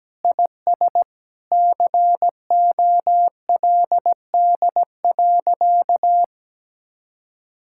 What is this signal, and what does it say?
Morse code, heard as a keyed tone, about 17 wpm, 700 Hz. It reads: IS COLD.
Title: Sentences from Top 1600 Words 17wpm